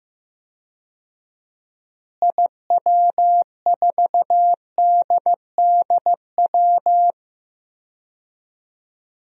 Morse code, heard as a keyed tone, about 15 wpm, 700 Hz. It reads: IW4DDW
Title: DXCC-100 Call Signs - 6 Characters - Repeated 2x - 15wpm